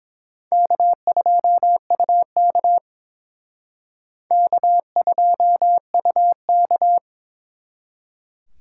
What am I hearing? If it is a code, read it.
Morse code, heard as a keyed tone, about 28 wpm, 700 Hz. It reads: K2UK K2UK